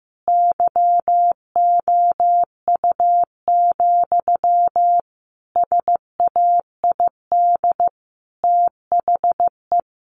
Morse code, heard as a keyed tone, about 15 wpm, 700 Hz. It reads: YOU, SAID THE
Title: Sentences from Top 600 Words 15wpm